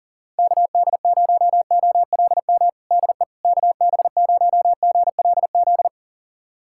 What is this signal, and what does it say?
Morse code, heard as a keyed tone, about 37 wpm, 700 Hz. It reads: KD0OLM DE KB0GLZ
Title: USA Callsign DE Callsign - 6 Characters - 40wpm